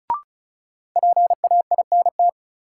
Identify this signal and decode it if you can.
Morse code, heard as a keyed tone, about 36 wpm, 700 Hz. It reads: PAINT
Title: Sentences from Top 1000 Words - Review Part 6 - New Word Drill - 35wpm